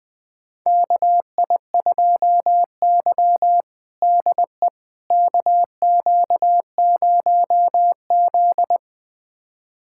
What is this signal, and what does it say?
Morse code, heard as a keyed tone, about 20 wpm, 700 Hz. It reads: KI2Y DE KQ0Z
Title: USA Callsign DE Callsign - 4 Characters - 20wpm